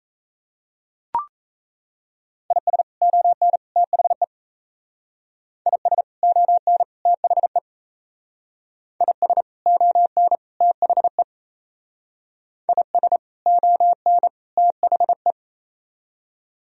Morse code, heard as a keyed tone, about 35 wpm, 700 Hz. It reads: IS ON THE IS ON THE IS ON THE IS ON THE
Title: Speed-Racing - Sets of 3 Words 28wpm